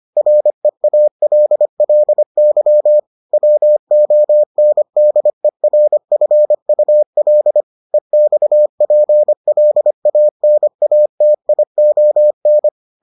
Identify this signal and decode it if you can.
Morse code, heard as a keyed tone, about 25 wpm, 600 Hz. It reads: REALLY WONDERFUL EXPLANATION